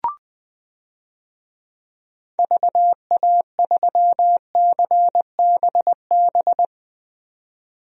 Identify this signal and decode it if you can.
Morse code, heard as a keyed tone, about 20 wpm, 700 Hz. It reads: VA3CBB
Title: DXCC-100 Call Signs - 6 Characters - 20wpm